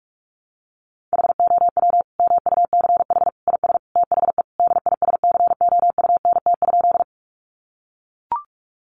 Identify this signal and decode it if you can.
Morse code, heard as a keyed tone, about 44 wpm, 700 Hz. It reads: HOW MUCH IS THE DISCOUNT?
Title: Sentences from Top 1200 Words 45wpm